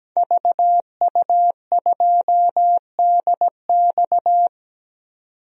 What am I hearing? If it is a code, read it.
Morse code, heard as a keyed tone, about 17 wpm, 700 Hz. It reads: VU2DX
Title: DXCC-100 Call Signs - 5 Characters - 17wpm